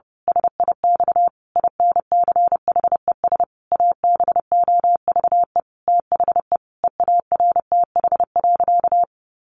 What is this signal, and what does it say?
Morse code, heard as a keyed tone, about 30 wpm, 700 Hz. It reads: SIX INCHES ABOVE THE EARTH.
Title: A Sound Of Thunder 30wpm